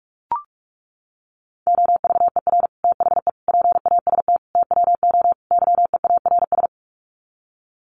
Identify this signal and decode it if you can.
Morse code, heard as a keyed tone, about 44 wpm, 700 Hz. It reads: OVER THE PAST TWO YEARS
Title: Sets of 5 Words Encore 45wpm